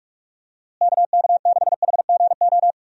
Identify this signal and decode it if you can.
Morse code, heard as a keyed tone, about 46 wpm, 700 Hz. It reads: KK6HGO
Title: USA Call Signs - 6 Characters - Repeated 2x - 45wpm